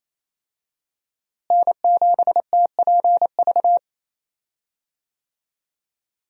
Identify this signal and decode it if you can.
Morse code, heard as a keyed tone, about 28 wpm, 700 Hz. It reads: N7TPV
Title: USA Call Signs - 5 Characters - Repeated 2x - 28wpm